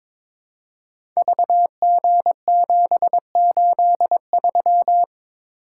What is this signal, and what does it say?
Morse code, heard as a keyed tone, about 22 wpm, 700 Hz. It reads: VG783
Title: Code Groups - 5 Letter:Numbers - 22wpm